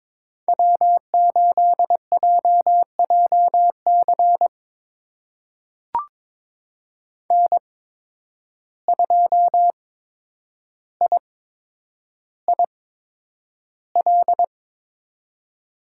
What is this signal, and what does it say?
Morse code, heard as a keyed tone, about 21 wpm, spaced as for 9 wpm, 700 Hz. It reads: W8JJC N2IIL
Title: USA Call Signs - 5 Characters - 8x Character Spacing - 22wpm